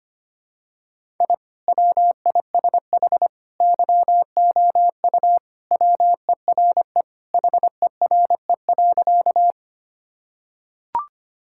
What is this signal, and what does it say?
Morse code, heard as a keyed tone, about 25 wpm, 700 Hz. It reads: I WISH YOU WERE HERE.
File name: practice / 25wpm / Sentences with 2-Word Phrases - 25wpm